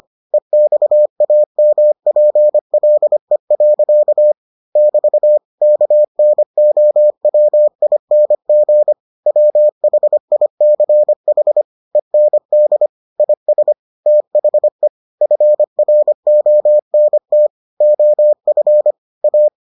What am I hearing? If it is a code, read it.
Morse code, heard as a keyed tone, about 25 wpm, 600 Hz. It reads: EXAMPLE. <BT> KNOWING WHICH END IS THE FRONT OF A